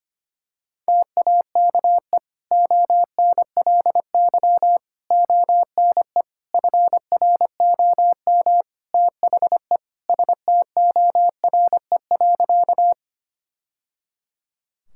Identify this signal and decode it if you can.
Morse code, heard as a keyed tone, about 25 wpm, 700 Hz. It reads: TAKE ONLY ONE FROM THE STORE.
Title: Sentences from Top 1000 Words - Review Part 5 - 25wpm